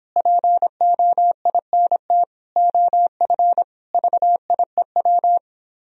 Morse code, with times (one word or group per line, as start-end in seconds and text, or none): 0.2-2.2 POINT
2.6-3.6 OF
3.9-5.4 VIEW